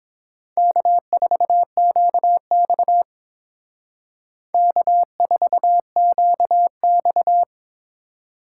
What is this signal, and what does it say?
Morse code, heard as a keyed tone, about 29 wpm, 700 Hz. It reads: K4QX K4QX